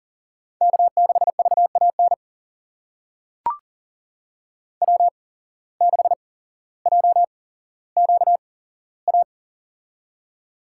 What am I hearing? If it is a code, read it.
Morse code, heard as a keyed tone, about 40 wpm, spaced as for 17 wpm, 700 Hz. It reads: K6VAN W6JQA
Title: USA Call Signs - 5 Characters - 8x Character Spacing - 40wpm